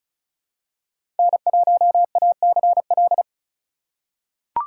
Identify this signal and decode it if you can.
Morse code, heard as a keyed tone, about 36 wpm, 700 Hz. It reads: N1ACL
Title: USA Call Signs - 5 Characters - 2x Character Spacing - 35wpm